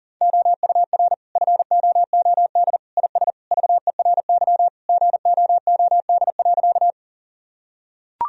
Morse code, heard as a keyed tone, about 40 wpm, 700 Hz. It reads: OUR FOOD IS VERY GOOD.
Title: Sentences from Top 1000 Words - Review Part 1 - 40wpm